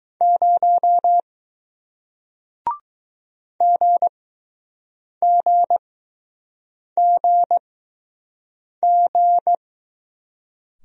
Morse code, heard as a keyed tone, about 20 wpm, spaced as for 6 wpm, 700 Hz. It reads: 0 GGGG